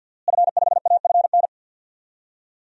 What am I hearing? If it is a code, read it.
Morse code, heard as a keyed tone, about 47 wpm, 700 Hz. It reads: F5AFN
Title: DXCC-100 Call Signs - 5 Characters - Repeated 3x - 50wpm